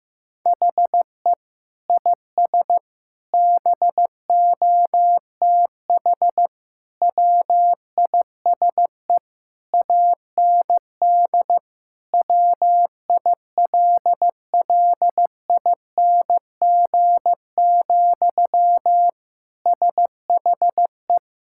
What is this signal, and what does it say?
Morse code, heard as a keyed tone, about 16 wpm, 700 Hz. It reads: HE IS BOTH WISE AND WILLING, SHE
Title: Sentences from Top 1600 Words 15wpm